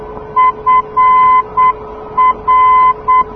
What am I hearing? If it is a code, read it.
Morse code, heard as a keyed tone, about 8 wpm, 1020 Hz. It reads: FR
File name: europe / d-6028 - ndb frankfurt fr-297 (2004) (rainer brannolte, d) xc